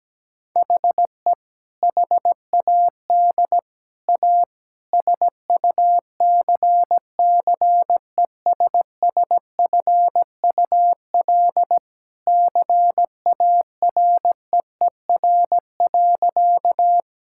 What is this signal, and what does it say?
Morse code, heard as a keyed tone, about 17 wpm, 700 Hz. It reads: HE HAD A SUCCESSFUL CAREER.
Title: Sentences from Top 1000 Words - Review Part 6 - 17wpm